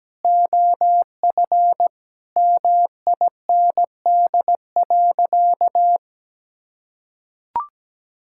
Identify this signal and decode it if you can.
Morse code, heard as a keyed tone, about 17 wpm, 700 Hz. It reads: OF MIND.
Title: Sentences from Top 1000 Words - Review Part 6 - 17wpm